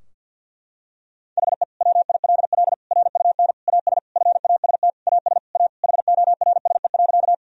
Morse code, heard as a keyed tone, about 51 wpm, 700 Hz. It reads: HE WILL RUN AS FAST AS A HORSE.
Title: Sentences from Top 1000 Words - Review Part 2 - 50wpm